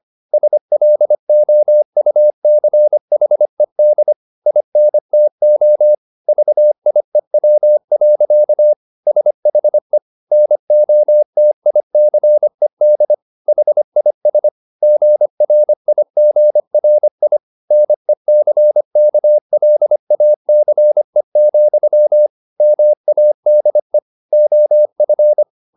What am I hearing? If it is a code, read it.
Morse code, heard as a keyed tone, about 25 wpm, 600 Hz. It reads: SLOUCHED INTO VIEW. SHE NOTICED HIS GRIGRI NECKLACE, MADE OF